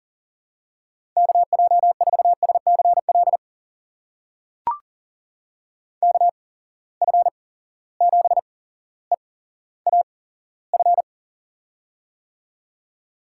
Morse code, heard as a keyed tone, about 38 wpm, spaced as for 17 wpm, 700 Hz. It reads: KJ4SCL KF7EAF
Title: USA Call Signs - 6 Characters - 8x Character Spacing - 40wpm